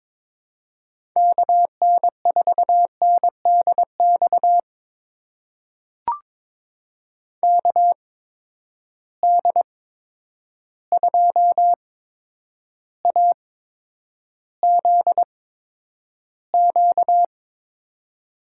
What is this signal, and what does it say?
Morse code, heard as a keyed tone, about 22 wpm, spaced as for 9 wpm, 700 Hz. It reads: KN4NDX KD2AZQ
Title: USA Call Signs - 6 Characters - 8x Character Spacing - 22wpm